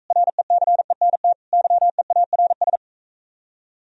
Morse code, heard as a keyed tone, about 43 wpm, 700 Hz. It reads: RECENT YEARS